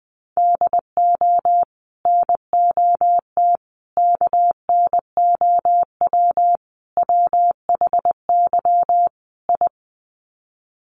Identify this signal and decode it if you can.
Morse code, heard as a keyed tone, about 20 wpm, 700 Hz. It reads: DO NOT KNOW WHY I
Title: Sets of 5 Words Encore 20wpm